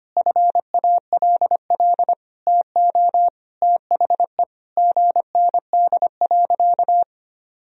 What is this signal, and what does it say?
Morse code, heard as a keyed tone, about 25 wpm, 700 Hz. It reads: FALL TO THE GND.
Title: Sentences from Top 500 Words and Abbreviations 25wpm